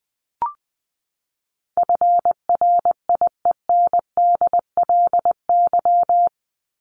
Morse code, heard as a keyed tone, about 20 wpm, 700 Hz. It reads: FRIENDLY
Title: LY Suffix - Top 100 Words with Base Words 20wpm